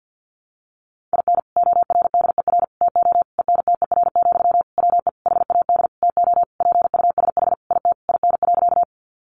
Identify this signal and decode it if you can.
Morse code, heard as a keyed tone, about 50 wpm, 700 Hz. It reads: IN ORDER TO ENTER, WE HAD TO PUSH IT IN.
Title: Sentences from Top 1300 Words 50wpm